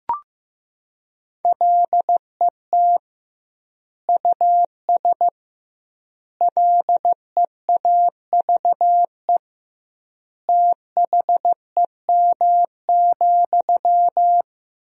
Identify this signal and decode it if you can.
Morse code, heard as a keyed tone, about 15 wpm, 700 Hz. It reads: LET US LEAVE THEM,
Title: 2x Word-Spacing - Sentences from Top 400 Words - 15wpm